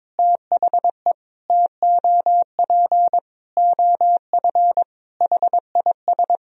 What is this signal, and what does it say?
Morse code, heard as a keyed tone, about 21 wpm, 700 Hz. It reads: THE TOP OF HIS
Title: Sentences from Top 1000 Words - Review Part 2 - 22wpm